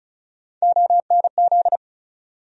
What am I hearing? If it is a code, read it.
Morse code, heard as a keyed tone, about 36 wpm, 700 Hz. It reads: ONZ
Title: USA Call Sign Postfixes - 3 Letters 35wpm